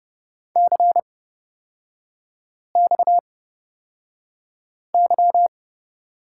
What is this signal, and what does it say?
Morse code, heard as a keyed tone, about 30 wpm, spaced as for 5 wpm, 700 Hz. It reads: CXY
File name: Lesson 37e - Letter x - Three Characters 30wpm